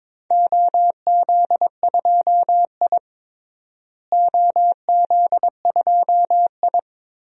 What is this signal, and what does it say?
Morse code, heard as a keyed tone, about 22 wpm, 700 Hz. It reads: OZ2I OZ2I
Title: DXCC-100 Call Signs - 4 Characters - Repeated 2x - 22wpm